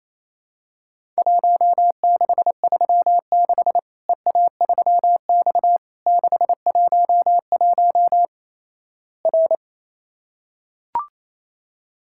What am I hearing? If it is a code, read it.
Morse code, heard as a keyed tone, about 28 wpm, 700 Hz. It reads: 1636 EA3X 611 R
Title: Worked All Europe DX Contest - QTC Exchange - 28wpm